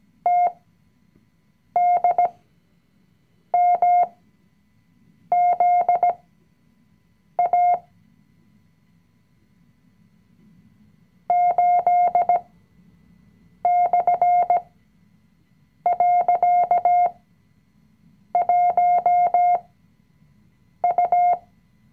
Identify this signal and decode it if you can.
Morse code, heard as a keyed tone, about 17 wpm, spaced as for 6 wpm, 710 Hz. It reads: TDMZA 8/.1U
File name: CW-Online-Kurs / Anfänger / Lektion 020